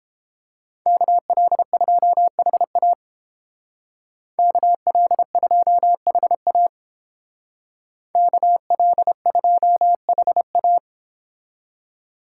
Morse code, heard as a keyed tone, about 29 wpm, 700 Hz. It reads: KL2HA KL2HA KL2HA